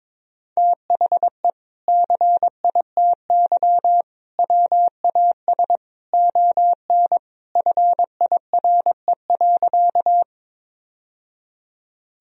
Morse code, heard as a keyed tone, about 22 wpm, 700 Hz. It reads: THE CITY WAS ON FIRE.
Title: Sentences from Top 1000 Words - Review Part 1 - 22wpm